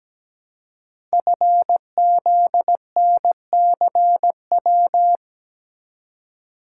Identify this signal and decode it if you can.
Morse code, heard as a keyed tone, about 17 wpm, 700 Hz. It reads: FZNCW
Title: Code Groups - 5 Letters - 17wpm